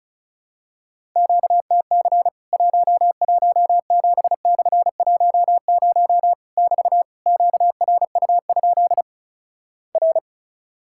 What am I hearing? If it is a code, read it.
Morse code, heard as a keyed tone, about 36 wpm, 700 Hz. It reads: QTC 117/10 <BT> QRU? R
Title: Worked All Europe DX Contest - QTC Exchange - 35wpm